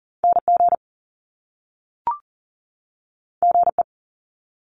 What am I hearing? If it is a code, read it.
Morse code, heard as a keyed tone, about 40 wpm, 700 Hz. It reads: NG GE
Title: Top Two Letter Combinations in Words 40wpm